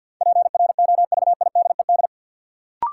Morse code, heard as a keyed tone, about 53 wpm, 700 Hz. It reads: PROVIDED
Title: Sentences from Top 1500 Words - New Word Drill - 50wpm